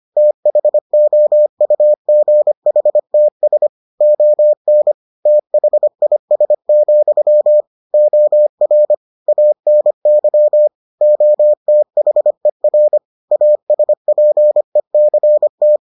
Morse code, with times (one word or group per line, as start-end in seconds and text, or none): 0.2-3.7 THOUGHTS
4.0-4.9 ON
5.3-7.6 THIS,
7.9-8.9 OR
9.3-10.7 ANY
11.0-13.0 OTHER
13.3-15.8 ASPECT